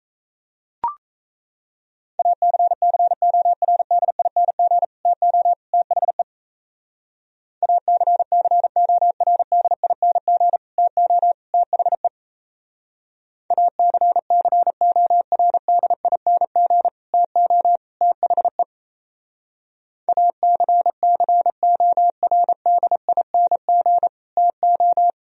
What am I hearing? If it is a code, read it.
Morse code, heard as a keyed tone, about 34 wpm, 700 Hz. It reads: ACCORDING TO THE ACCORDING TO THE ACCORDING TO THE ACCORDING TO
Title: Speed-Racing - Sets of 3 Words 28wpm